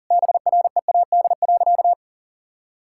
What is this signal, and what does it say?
Morse code, heard as a keyed tone, about 40 wpm, 700 Hz. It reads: BREAD.
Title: Sentences from Top 1600 Words 40wpm